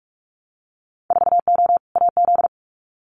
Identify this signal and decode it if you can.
Morse code, heard as a keyed tone, about 46 wpm, 700 Hz. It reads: 4O AZ
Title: Winter Field Day Exchanges Encore 45wpm